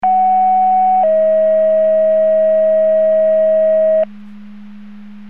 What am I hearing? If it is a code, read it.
Two-tone page, heard as a one-second tone, then a three-second tone, about 750 then 640 Hz. It is heard from 0.0 to 4.0 s.